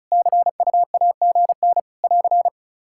Morse code, heard as a keyed tone, about 36 wpm, 700 Hz. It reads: CUAGN <AR>
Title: Practice QSO - Exchange 4 - Ending - 35wpm